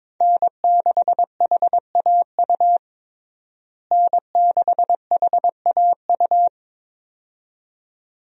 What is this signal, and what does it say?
Morse code, heard as a keyed tone, about 21 wpm, 700 Hz. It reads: N6HAU N6HAU
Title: USA Call Signs - 5 Characters - Repeated 2x - 22wpm